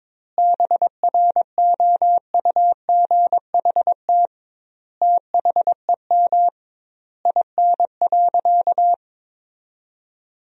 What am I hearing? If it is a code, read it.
Morse code, heard as a keyed tone, about 22 wpm, 700 Hz. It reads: BROUGHT THEM IN.
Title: 2x Word-Spacing - Sentences from Top 500 Words - 22wpm